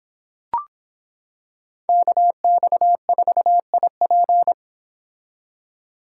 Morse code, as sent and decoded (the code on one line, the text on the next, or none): -.- -..- ....- .. .--.
KX4IP